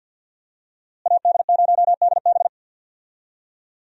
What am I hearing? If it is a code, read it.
Morse code, heard as a keyed tone, about 52 wpm, 700 Hz. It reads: AD0DB
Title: USA Call Signs - 5 Characters - Repeated 2x - 50wpm